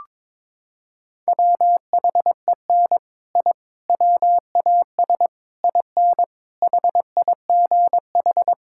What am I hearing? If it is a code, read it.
Morse code, heard as a keyed tone, about 22 wpm, 700 Hz. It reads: WHEN I WAS IN HIGH
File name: practice / 22wpm / Sets of 6 Words - 22wpm